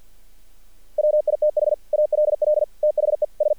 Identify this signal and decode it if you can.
Morse code, heard as a keyed tone, about 48 wpm, 600 Hz. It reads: WITH ALL THE S
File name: archived / CRSnet